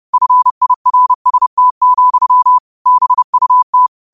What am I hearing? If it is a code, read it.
Morse code, heard as a keyed tone, about 30 wpm, 1000 Hz. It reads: FIRST, BUT
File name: qotdfiles / 30wpm